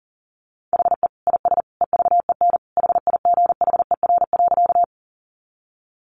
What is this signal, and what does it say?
Morse code, heard as a keyed tone, about 40 wpm, 700 Hz. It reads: HE IS EVEN HIGHER.
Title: Additional Easy Phrases 40wpm